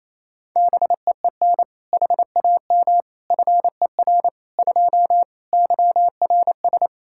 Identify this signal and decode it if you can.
Morse code, heard as a keyed tone, about 28 wpm, 700 Hz. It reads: BEEN HAM FER 2 YRS
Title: K1USN SST QSO Practice 28wpm